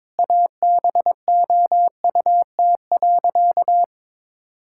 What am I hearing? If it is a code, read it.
Morse code, heard as a keyed tone, about 22 wpm, 700 Hz. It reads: ABOUT.
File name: practice / 22wpm / Sentences with 8-Word Phrases - 22wpm